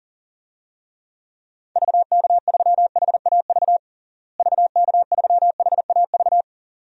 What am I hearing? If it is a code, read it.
Morse code, heard as a keyed tone, about 37 wpm, 700 Hz. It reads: VK3HAV VK3HAV
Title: DXCC-100 Call Signs - 6 Characters - Repeated 2x - 40wpm